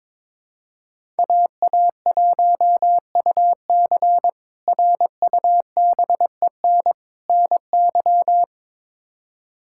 Morse code, heard as a keyed tone, about 22 wpm, 700 Hz. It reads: AA1UC RUBEN NY
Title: K1USN SST QSO Practice - 22wpm